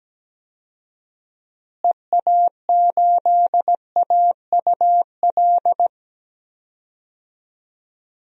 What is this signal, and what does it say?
Morse code, heard as a keyed tone, about 17 wpm, 700 Hz. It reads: EA8AUL
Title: DXCC-100 Call Signs - 6 Characters - Repeated 3x - 17wpm